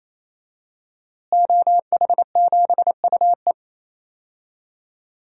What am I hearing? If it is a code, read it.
Morse code, heard as a keyed tone, about 27 wpm, 700 Hz. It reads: OH7UE
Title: DXCC-100 Call Signs - 5 Characters - Repeated 3x - 28wpm